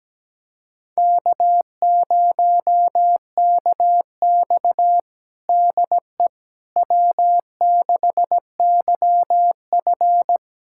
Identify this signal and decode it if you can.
Morse code, heard as a keyed tone, about 17 wpm, 700 Hz. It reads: K0KX DE W6YF
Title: USA Callsign DE Callsign - 4 Characters - 17wpm